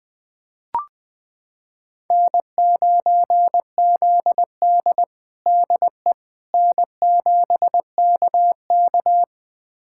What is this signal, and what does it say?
Morse code, heard as a keyed tone, about 20 wpm, 700 Hz. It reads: N9ZD DE N7KK
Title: USA Callsign DE Callsign - 4 Characters - 20wpm